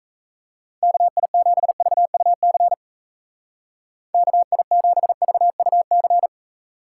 Morse code, heard as a keyed tone, about 33 wpm, 700 Hz. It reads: KI7VUC KI7VUC
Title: Speed Racing - USA 2x3 Call Signs 28wpm